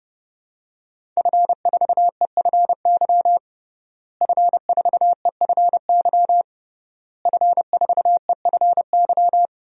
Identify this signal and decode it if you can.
Morse code, heard as a keyed tone, about 30 wpm, 700 Hz. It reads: F4EFY F4EFY F4EFY
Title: DXCC-100 Call Signs - 5 Characters - Repeated 3x - 30wpm